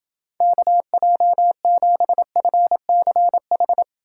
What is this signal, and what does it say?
Morse code, heard as a keyed tone, about 28 wpm, 700 Hz. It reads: KJ7FCH